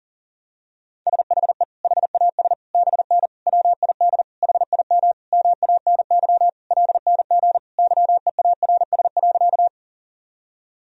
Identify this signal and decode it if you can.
Morse code, heard as a keyed tone, about 37 wpm, 700 Hz. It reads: SHE HAS BN WID HIM MANY LNG YEARS.
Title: Sentences from Top 200 Words and Abbreviations 40wpm